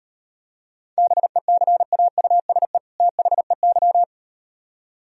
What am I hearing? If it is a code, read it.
Morse code, heard as a keyed tone, about 38 wpm, 700 Hz. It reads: BECAUSE THEY